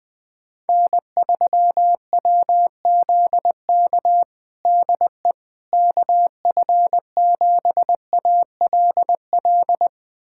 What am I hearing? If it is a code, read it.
Morse code, heard as a keyed tone, about 21 wpm, 700 Hz. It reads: N3WZK DE KF7ALL
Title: USA Callsign DE Callsign 20wpm